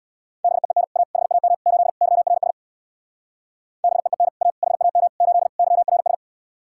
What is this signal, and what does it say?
Morse code, heard as a keyed tone, about 64 wpm, 700 Hz. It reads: BEEN A HAM 8 YRS BEEN A HAM 8 YRS